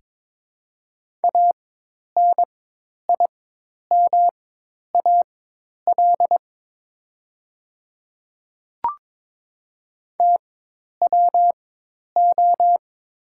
Morse code, heard as a keyed tone, about 22 wpm, spaced as for 10 wpm, 690 Hz. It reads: ANIMAL TWO
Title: 4x Character Spacing - Top 200 Words - 22wpm